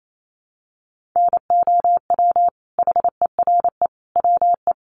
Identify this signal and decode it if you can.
Morse code, heard as a keyed tone, about 28 wpm, 700 Hz. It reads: NOW HERE WE
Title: Sentences from Top 1200 Words 28wpm